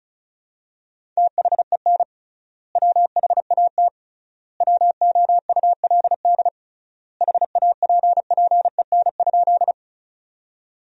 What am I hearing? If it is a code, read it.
Morse code, heard as a keyed tone, about 34 wpm, 700 Hz. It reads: THEN WHAT WOULD HAPPEN?
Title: 3x Word-Spacing - Sentences from Top 400 Words - 35wpm